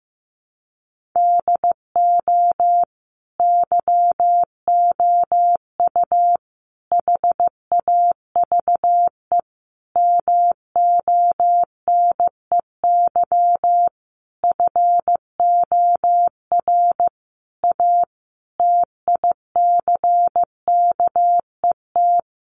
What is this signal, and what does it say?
Morse code, heard as a keyed tone, about 15 wpm, 700 Hz. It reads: DO YOU HAVE MONEY FOR A TICKET
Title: Sentences from Top 1400 Words 15wpm